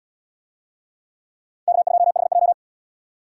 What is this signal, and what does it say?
Morse code, heard as a keyed tone, about 78 wpm, 700 Hz. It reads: K2HJ